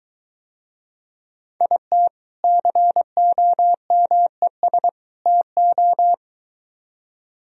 Morse code, heard as a keyed tone, about 23 wpm, 700 Hz. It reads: IT COMES TO